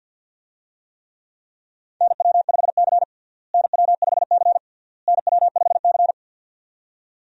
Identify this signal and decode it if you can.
Morse code, heard as a keyed tone, about 52 wpm, 700 Hz. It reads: NW5C NW5C NW5C
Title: USA Call Signs - 4 Characters - Repeated 3x - 50wpm